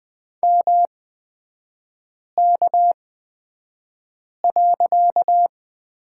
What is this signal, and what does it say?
Morse code, heard as a keyed tone, about 20 wpm, spaced as for 5 wpm, 700 Hz. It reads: MK.
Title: 255 Lesson 41e - Period - Three Characters 20wpm